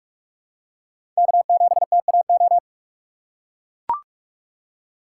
Morse code, heard as a keyed tone, about 41 wpm, 700 Hz. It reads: K7TAO
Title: USA Call Signs - 5 Characters - 4x Character Spacing - 45wpm